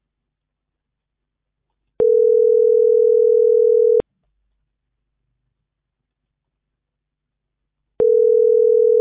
Ringback tone (North America), 440 + 480 Hz.